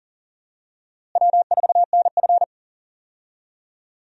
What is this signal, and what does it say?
Morse code, heard as a keyed tone, about 40 wpm, 700 Hz. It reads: W4NF
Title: USA Call Signs - 4 Characters - Repeated 2x - 40wpm